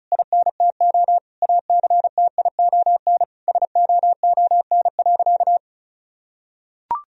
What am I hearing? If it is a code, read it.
Morse code, heard as a keyed tone, about 34 wpm, 700 Hz. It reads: INTO ACTION SOON.
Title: Sentences from Top 1000 Words - Review Part 4 - 35wpm